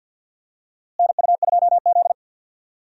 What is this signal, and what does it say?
Morse code, heard as a keyed tone, about 46 wpm, 700 Hz. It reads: NU2Z
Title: USA Call Signs - 4 Characters - Repeated 3x - 50wpm